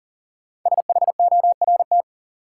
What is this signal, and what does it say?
Morse code, heard as a keyed tone, about 40 wpm, 700 Hz. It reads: SHORT
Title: Sets of 5 Words Encore - New Word Drill - 40wpm